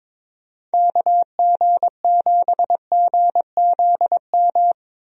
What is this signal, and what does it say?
Morse code, heard as a keyed tone, about 22 wpm, 700 Hz. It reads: KG7GZM
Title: USA Call Signs - 6 Characters - Repeated 2x - 22wpm